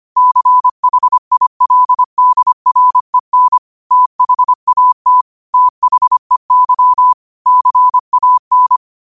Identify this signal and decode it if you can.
Morse code, heard as a keyed tone, about 25 wpm, 1000 Hz. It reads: CHILDREN THAT THEY CAN
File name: qotdfiles / 25wpm